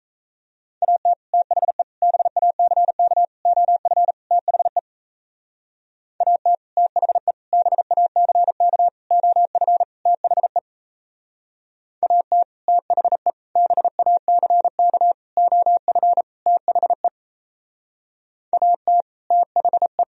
Morse code, heard as a keyed tone, about 34 wpm, 700 Hz. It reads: AT THE BACK OF THE AT THE BACK OF THE AT THE BACK OF THE AT THE
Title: Speed-Racing - Sets of 5 Words 28wpm